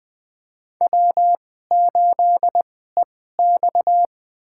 Morse code, heard as a keyed tone, about 20 wpm, spaced as for 15 wpm, 700 Hz. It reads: W8EX
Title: USA Call Signs - 4 Characters - 2x Character Spacing - 20wpm